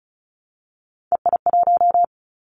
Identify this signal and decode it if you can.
Morse code, heard as a keyed tone, about 36 wpm, 700 Hz. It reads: EI1
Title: Callsign Prefixes for Top 50 DXCC - 35wpm